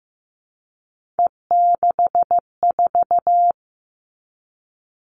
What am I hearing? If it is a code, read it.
Morse code, heard as a keyed tone, about 15 wpm, 700 Hz. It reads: E64
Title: Three Challenging Letter-Number Combinations 15wpm